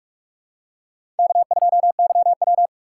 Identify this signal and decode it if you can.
Morse code, heard as a keyed tone, about 43 wpm, 700 Hz. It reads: K2YW
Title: USA Call Signs - 4x Character Spacing - 45wpm